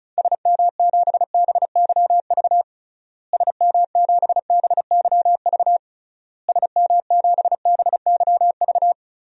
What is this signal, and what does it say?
Morse code, heard as a keyed tone, about 36 wpm, 700 Hz. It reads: SM7BYV SM7BYV SM7BYV
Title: DXCC-100 Call Signs - 6 Characters - Repeated 3x - 35wpm